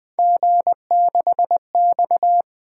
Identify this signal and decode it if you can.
Morse code, heard as a keyed tone, about 20 wpm, 700 Hz. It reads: G6X
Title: Three Letter-Number Combinations - 20wpm